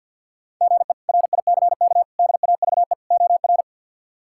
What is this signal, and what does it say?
Morse code, heard as a keyed tone, about 52 wpm, 700 Hz. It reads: GE RICK DAVE OR
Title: K1USN SST QSO Practice - 50wpm